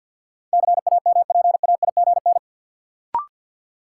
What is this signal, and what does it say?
Morse code, heard as a keyed tone, about 49 wpm, 700 Hz. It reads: CAMPAIGN
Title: Sentences from Top 1200 Words - New Word Drill Alt - 50wpm